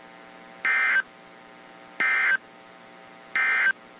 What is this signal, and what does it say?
Emergency Alert System (SAME) tones, heard as harsh buzzing bursts. It is heard from 0.5 to 3.8 s.
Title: WZ2523 - NOAA Weather Radio